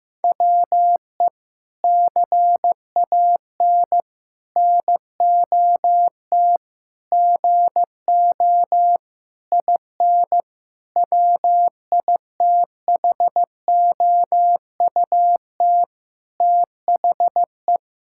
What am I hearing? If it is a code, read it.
Morse code, heard as a keyed tone, about 15 wpm, 700 Hz. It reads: WE CAN NOT GO IN WITHOUT THE
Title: Sentences from Top 1000 Words - Review Part 6 - 15wpm